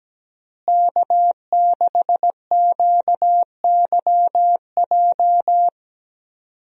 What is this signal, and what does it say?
Morse code, heard as a keyed tone, about 17 wpm, 700 Hz. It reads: K6QYJ
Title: USA Call Signs - 5 Characters - Repeated 3x - 17wpm